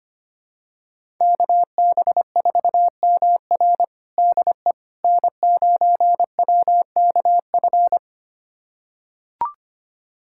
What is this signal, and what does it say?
Morse code, heard as a keyed tone, about 25 wpm, 700 Hz. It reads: KB4MR DE N9WKF
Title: USA Callsign DE Callsign - 5 Characters - 25wpm